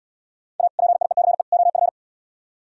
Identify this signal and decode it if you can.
Morse code, heard as a keyed tone, about 77 wpm, 700 Hz. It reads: A PIECE OF